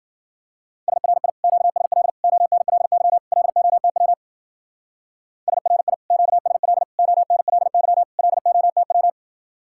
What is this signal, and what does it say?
Morse code, heard as a keyed tone, about 49 wpm, 700 Hz. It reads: SRI QSL ONLY LOTW SRI QSL ONLY LOTW
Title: Speed-Racing Common QSO Phrases - 40wpm